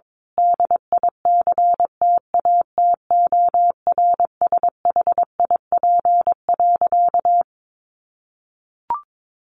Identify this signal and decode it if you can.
Morse code, heard as a keyed tone, about 22 wpm, 700 Hz. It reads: DICTATORSHIP.
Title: A Sound Of Thunder 22wpm